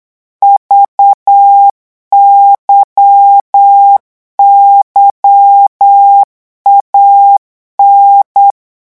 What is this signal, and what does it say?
Morse code, heard as a keyed tone, about 8 wpm, 800 Hz. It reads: VYYAN
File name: morse mp3 / revision3-vitesse 8 mots